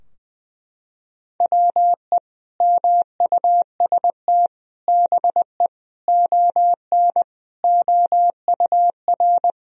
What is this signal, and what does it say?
Morse code, heard as a keyed tone, about 19 wpm, 700 Hz. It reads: WE MUST BE ON OUR
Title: Sentences from Top 1000 Words - Review Part 4 - 20wpm